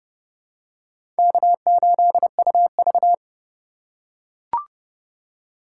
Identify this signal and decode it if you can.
Morse code, heard as a keyed tone, about 30 wpm, 700 Hz. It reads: K8UV
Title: USA Call Signs - 4 Characters - 2x Character Spacing - 30wpm